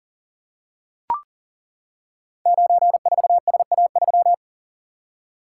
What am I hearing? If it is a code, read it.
Morse code, heard as a keyed tone, about 43 wpm, 700 Hz. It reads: 94SA3
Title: Code Groups - 5 Letter:Numbers - 40wpm